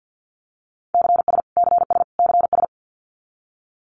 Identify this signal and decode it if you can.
Morse code, heard as a keyed tone, about 50 wpm, 700 Hz. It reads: CS CS CS
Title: Abbreviations Repeated 3x - 50wpm